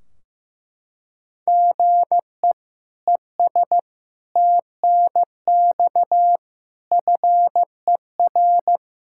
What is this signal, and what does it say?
Morse code, heard as a keyed tone, about 15 wpm, 700 Hz. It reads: GE ES TNX FER
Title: Practice QSO - Exchange 1 - Beginning - 15wpm